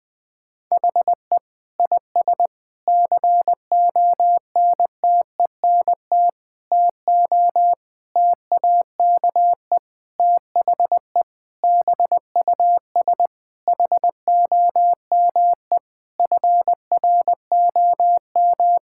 Morse code, heard as a keyed tone, about 20 wpm, 700 Hz. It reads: HE IS CONTENT TO TAKE THE BUS HOME FROM
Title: Sentences from Top 1000 Words - Review Part 6 - 20wpm